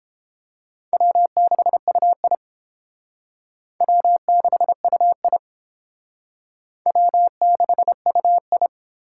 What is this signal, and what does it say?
Morse code, heard as a keyed tone, about 30 wpm, 700 Hz. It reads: W6UI W6UI W6UI